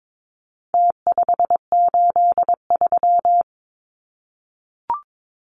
Morse code, heard as a keyed tone, about 22 wpm, 700 Hz. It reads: T583
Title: Numbers - Contest Style - 4-Digit 22wpm